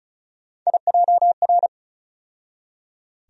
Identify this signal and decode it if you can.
Morse code, heard as a keyed tone, about 36 wpm, 700 Hz. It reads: IJR
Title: USA Call Sign Postfixes - 3 Letters 35wpm